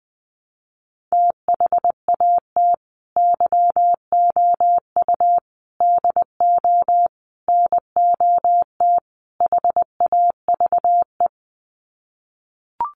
Morse code, heard as a keyed tone, about 20 wpm, 700 Hz. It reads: THAT YOU DO NOT HAVE
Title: Sets of 5 Words Encore 20wpm